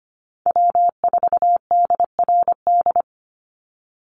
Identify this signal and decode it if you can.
Morse code, heard as a keyed tone, about 25 wpm, 700 Hz. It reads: W4DRD
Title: USA Call Signs 25wpm